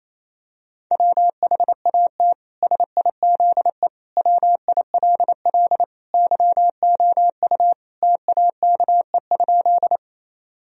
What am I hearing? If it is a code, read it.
Morse code, heard as a keyed tone, about 28 wpm, 700 Hz. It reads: WHAT SIZE WILL YOU TAKE?
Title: Sentences from Top 1000 Words - Review Part 2 - 28wpm